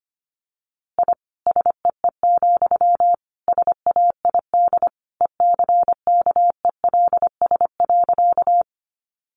Morse code, heard as a keyed tone, about 25 wpm, 700 Hz. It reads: I SEE, SAID ECKELS.
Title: A Sound Of Thunder 25wpm